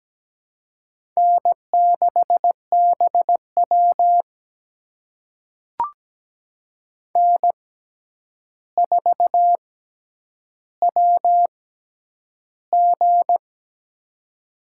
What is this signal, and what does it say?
Morse code, heard as a keyed tone, about 17 wpm, spaced as for 9 wpm, 700 Hz. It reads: N6BW N4WG
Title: USA Call Signs - 4 Characters - 6x Character Spacing - 17wpm